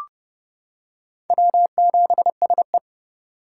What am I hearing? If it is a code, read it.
Morse code, heard as a keyed tone, about 32 wpm, 700 Hz. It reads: W7SE
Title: K1USN SST QSO Practice - 30wpm